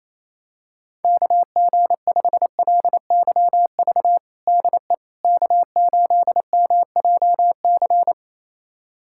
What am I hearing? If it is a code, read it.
Morse code, heard as a keyed tone, about 28 wpm, 700 Hz. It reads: KG5LYV DE K8MJC
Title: USA Callsign DE Callsign 28wpm